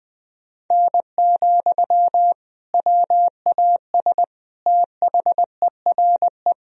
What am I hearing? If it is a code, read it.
Morse code, heard as a keyed tone, about 20 wpm, 700 Hz. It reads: N, WAS THERE
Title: Sentences from Top 100 Words and Abbreviations 20wpm